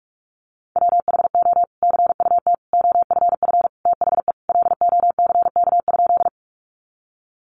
Morse code, heard as a keyed tone, about 44 wpm, 700 Hz. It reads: WHO CUT OFF THE LOCK?
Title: Sentences from Top 1300 Words 45wpm